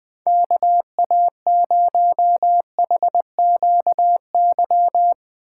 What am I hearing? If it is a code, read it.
Morse code, heard as a keyed tone, about 20 wpm, 700 Hz. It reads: KA0HQY
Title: Speed Racing - USA 2x3 Call Signs 17wpm